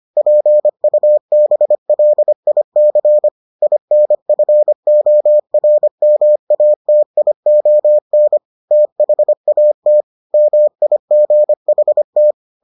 Morse code, heard as a keyed tone, about 25 wpm, 600 Hz. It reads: PUBLIC INFORMATION THAT MIGHT